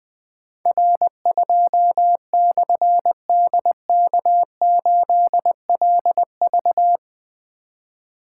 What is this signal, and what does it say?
Morse code, heard as a keyed tone, about 20 wpm, 700 Hz. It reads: R2/DK8LV
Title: Difficult Call Signs 20wpm